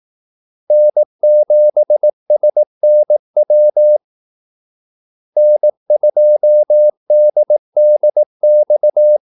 Morse code, heard as a keyed tone, about 18 wpm, 600 Hz. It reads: N7SNW N2DDX
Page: Kaldesignaler 18 wpm